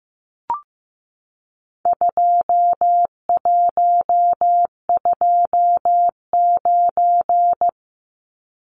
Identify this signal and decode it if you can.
Morse code, heard as a keyed tone, about 15 wpm, 700 Hz. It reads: 2129
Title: Numbers - 4-Digit 15wpm